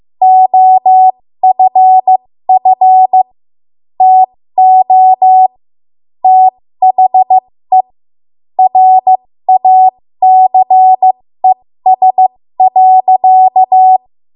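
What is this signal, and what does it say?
Morse code, heard as a keyed tone, about 15 wpm, spaced as for 13 wpm, 750 Hz. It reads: OFF TO THE RACES.
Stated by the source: 13 wpm Morse code practice archive transmission.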